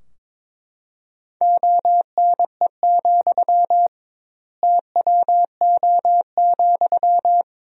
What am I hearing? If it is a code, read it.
Morse code, heard as a keyed tone, about 22 wpm, 700 Hz. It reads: ONE, TWO,
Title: 2x Word-Spacing - Sentences from Top 200 Words - 22wpm